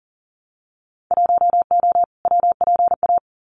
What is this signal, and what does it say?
Morse code, heard as a keyed tone, about 40 wpm, 700 Hz. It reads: 1O WPA
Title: Winter Field Day Exchanges 40wpm